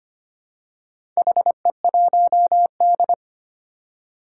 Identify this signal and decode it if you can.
Morse code, heard as a keyed tone, about 25 wpm, 700 Hz. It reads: HE1D
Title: Four Challenging Letter-Number Combinations 25wpm